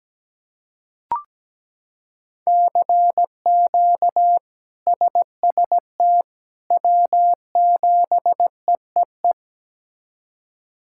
Morse code, heard as a keyed tone, about 17 wpm, 700 Hz. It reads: CQ SST W7EEE
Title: K1USN SST QSO Practice - 17wpm